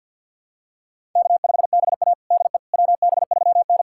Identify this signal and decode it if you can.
Morse code, heard as a keyed tone, about 53 wpm, 700 Hz. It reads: K5BA DE WB3N
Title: USA Callsign DE Callsign - 4 Characters - 50wpm